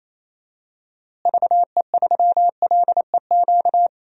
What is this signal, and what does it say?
Morse code, heard as a keyed tone, about 28 wpm, 700 Hz. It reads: VE3LEQ
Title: DXCC-100 Call Signs - 6 Characters - Repeated 3x - 28wpm